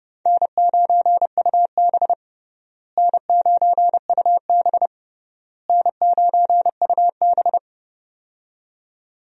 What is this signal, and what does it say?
Morse code, heard as a keyed tone, about 31 wpm, 700 Hz. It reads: N9UB N9UB N9UB
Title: USA Call Signs - 4 Characters - Repeated 3x - 30wpm